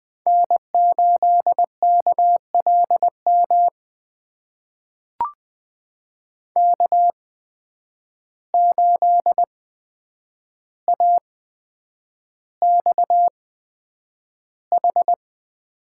Morse code, heard as a keyed tone, about 20 wpm, spaced as for 9 wpm, 700 Hz. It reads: N8KLM K8AXH
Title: USA Call Signs - 5 Characters - 8x Character Spacing - 20wpm